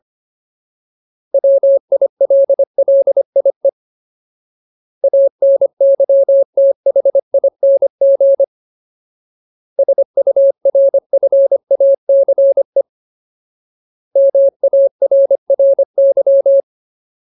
Morse code, heard as a keyed tone, about 25 wpm, 550 Hz. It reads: WILLIE ANYTHING SURFACE MARRY
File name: Ultimate Word List / 25WPM Ultimate